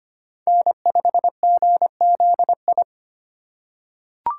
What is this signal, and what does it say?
Morse code, heard as a keyed tone, about 25 wpm, 700 Hz. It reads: N5GZI
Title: USA Call Signs - 5 Characters - 2x Character Spacing - 25wpm